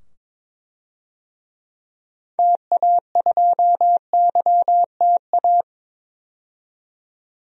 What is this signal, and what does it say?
Morse code, heard as a keyed tone, about 21 wpm, 700 Hz. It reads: TA2YTA
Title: DXCC-100 Call Signs - 6 Characters - 22wpm